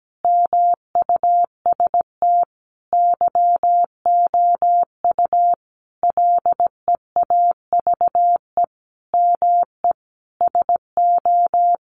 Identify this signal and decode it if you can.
Morse code, heard as a keyed tone, about 17 wpm, 700 Hz. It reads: MUST YOU LEAVE ME SO
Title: Sentences from Top 400 Words 17wpm